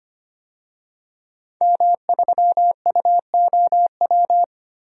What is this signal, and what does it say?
Morse code, heard as a keyed tone, about 25 wpm, 700 Hz. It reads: M3UOW
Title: DXCC-100 Call Signs - 5 Characters - 25wpm